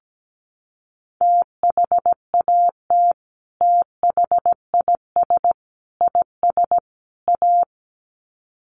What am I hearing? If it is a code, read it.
Morse code, heard as a keyed tone, about 17 wpm, 700 Hz. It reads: THAT THIS IS A
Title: Sets of 4 Words 17wpm